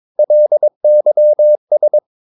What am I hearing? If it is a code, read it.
Morse code, heard as a keyed tone, about 22 wpm, 600 Hz. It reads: LYS
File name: Ferie QRS 22wpm